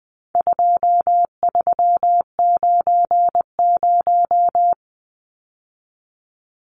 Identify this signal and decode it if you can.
Morse code, heard as a keyed tone, about 20 wpm, 700 Hz. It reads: 2390
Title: Numbers - 4-Digit 20wpm